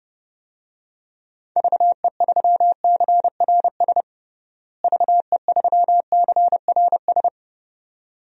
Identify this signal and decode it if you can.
Morse code, heard as a keyed tone, about 32 wpm, 700 Hz. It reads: VE3CRS VE3CRS
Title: DXCC-100 Call Signs - 6 Characters - Repeated 2x - 30wpm